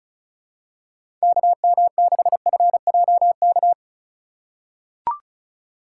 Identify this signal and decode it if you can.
Morse code, heard as a keyed tone, about 36 wpm, 700 Hz. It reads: KM6FJK
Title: USA Call Signs - 6 Characters - 2x Character Spacing - 35wpm